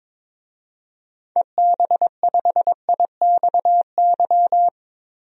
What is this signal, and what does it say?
Morse code, heard as a keyed tone, about 22 wpm, 700 Hz. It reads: EB5IXY
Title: DXCC-100 Call Signs - 6 Characters - Repeated 3x - 22wpm